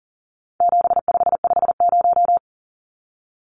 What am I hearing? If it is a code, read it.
Morse code, heard as a keyed tone, about 40 wpm, 700 Hz. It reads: 7550
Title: Numbers - 4-Digit 40wpm